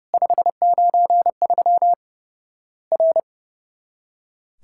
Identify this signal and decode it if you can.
Morse code, heard as a keyed tone, about 29 wpm, 700 Hz. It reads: 593 R
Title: Worked All Europe DX Contest - QTC Exchange - 30wpm